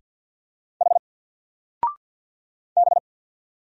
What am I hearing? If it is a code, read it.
Morse code, heard as a keyed tone, about 49 wpm, 700 Hz. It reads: H B
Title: Single Character (Letter:Number:Punctuation:Prosign) 50wpm